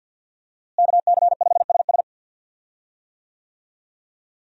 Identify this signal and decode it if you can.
Morse code, heard as a keyed tone, about 52 wpm, 700 Hz. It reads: KC5SS
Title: USA Call Signs - 5 Characters - Repeated 2x - 50wpm